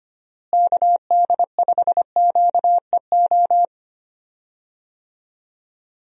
Morse code, heard as a keyed tone, about 25 wpm, 700 Hz. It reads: KD5QEO
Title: USA Call Signs - 6 Characters - Repeated 3x - 25wpm